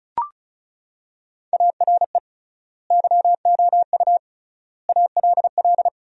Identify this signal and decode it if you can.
Morse code, heard as a keyed tone, about 36 wpm, 700 Hz. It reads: ARE YOU ALL
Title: 3x Word-Spacing - Sentences from Top 400 Words - 35wpm